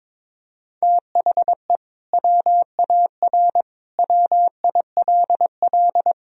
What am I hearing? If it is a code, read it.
Morse code, heard as a keyed tone, about 22 wpm, 700 Hz. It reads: THE WAR WILL
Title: Sentences from Top 1000 Words - Review Part 2 - 22wpm